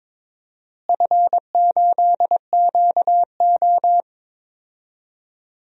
Morse code, heard as a keyed tone, about 21 wpm, 700 Hz. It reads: F8QO
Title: DXCC-100 Call Signs - 4 Characters - Repeated 3x - 22wpm